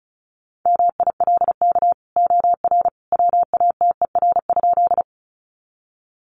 Morse code, heard as a keyed tone, about 34 wpm, 700 Hz. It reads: MILK OR WATER?
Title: Sentences from Top 1500 Words 35wpm